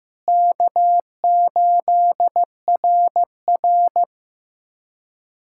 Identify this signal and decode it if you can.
Morse code, heard as a keyed tone, about 15 wpm, 700 Hz. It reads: K8RR
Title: K1USN SST QSO Practice - 15wpm